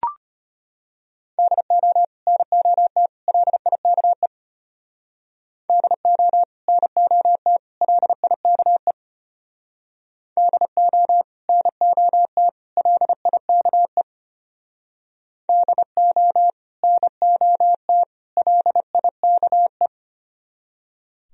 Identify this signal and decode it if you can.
Morse code, heard as a keyed tone, about 33 wpm, 700 Hz. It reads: DO NOT LIKE DO NOT LIKE DO NOT LIKE DO NOT LIKE